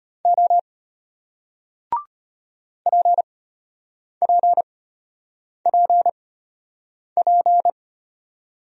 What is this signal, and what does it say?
Morse code, heard as a keyed tone, about 32 wpm, spaced as for 8 wpm, 700 Hz. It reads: O PPPP